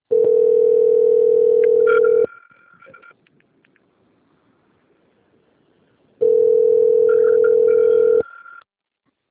Ringback tone (North America), 440 + 480 Hz.
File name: out- - - - - .27040